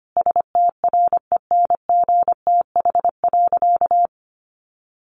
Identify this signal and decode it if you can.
Morse code, heard as a keyed tone, about 25 wpm, 700 Hz. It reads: STRENGTH.
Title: Sentences from Top 1300 Words 25wpm